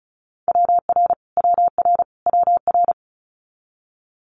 Morse code, heard as a keyed tone, about 34 wpm, 700 Hz. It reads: WR WR WR
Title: Multi-letter Phonemes Repeated 3x 35wpm